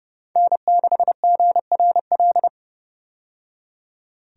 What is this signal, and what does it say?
Morse code, heard as a keyed tone, about 30 wpm, 700 Hz. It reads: N6GRL
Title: USA Call Signs - 5 Characters - 30wpm